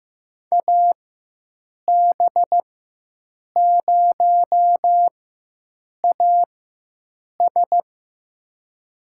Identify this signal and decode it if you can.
Morse code, heard as a keyed tone, about 15 wpm, spaced as for 7 wpm, 700 Hz. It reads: AB0AS
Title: USA Call Signs - 5 Characters - 4x Character Spacing - 15wpm